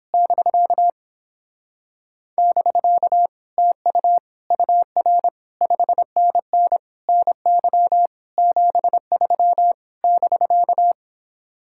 Morse code, heard as a keyed tone, about 26 wpm, 700 Hz. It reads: <BK> <BK> TU UR 5NN NY 73 <BK>
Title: Speed-Racing Common QSO Phrases - 22wpm